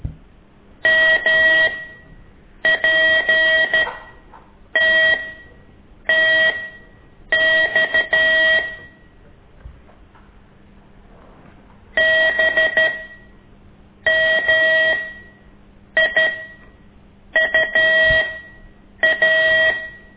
Morse code, heard as a keyed tone, about 13 wpm, spaced as for 7 wpm, 1900 Hz. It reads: MPTTX BMIUA
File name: ik1weg2 / CORSI CW